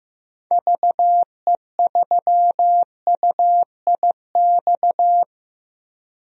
Morse code, heard as a keyed tone, about 15 wpm, 700 Hz. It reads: VE3UIX
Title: DXCC-100 Call Signs - 6 Characters - 15wpm